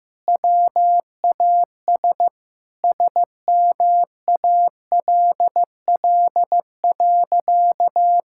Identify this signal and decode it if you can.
Morse code, heard as a keyed tone, about 15 wpm, 700 Hz. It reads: WAS SMALL.
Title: Sentences from Top 1000 Words - Review Part 5 - 15wpm